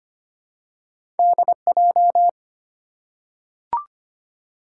Morse code, heard as a keyed tone, about 25 wpm, 700 Hz. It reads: DJ
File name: Two Letter-Number Combinations - 25wpm